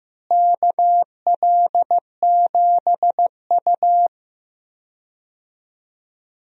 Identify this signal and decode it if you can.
Morse code, heard as a keyed tone, about 15 wpm, 700 Hz. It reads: KL7U
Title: DXCC-100 Call Signs - 4 Characters - Repeated 2x - 15wpm